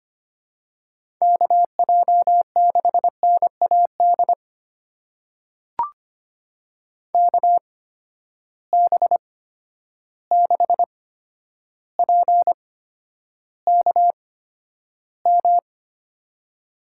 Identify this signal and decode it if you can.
Morse code, heard as a keyed tone, about 25 wpm, spaced as for 11 wpm, 700 Hz. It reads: KJ6NAD KB6PKM
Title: USA Call Signs - 6 Characters - 8x Character Spacing - 25wpm